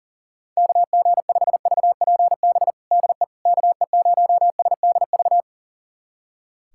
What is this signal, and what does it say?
Morse code, heard as a keyed tone, about 40 wpm, 700 Hz. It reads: KG5VPB DE KE0SDV
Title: USA Callsign DE Callsign 40wpm